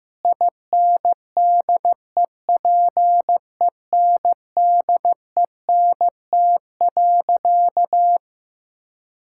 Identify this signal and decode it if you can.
Morse code, heard as a keyed tone, about 15 wpm, 700 Hz. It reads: INDEPENDENT.
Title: Sentences from Top 1000 Words - Review Part 6 - 15wpm